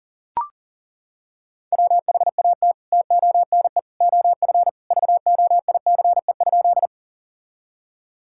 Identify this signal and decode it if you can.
Morse code, heard as a keyed tone, about 40 wpm, 700 Hz. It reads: WHAT TONE OF VOICE?
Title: Sentences from Top 1600 Words 40wpm